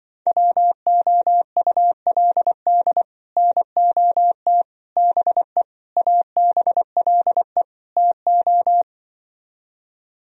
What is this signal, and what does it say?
Morse code, heard as a keyed tone, about 24 wpm, 700 Hz. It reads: WOULD NOT BE ABLE TO
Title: Speed-Racing - Sets of 5 Words 20wpm